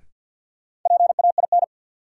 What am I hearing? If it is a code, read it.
Morse code, heard as a keyed tone, about 53 wpm, 700 Hz. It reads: PAIN
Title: Sentences from Top 1600 Words - New Word Drill - 50wpm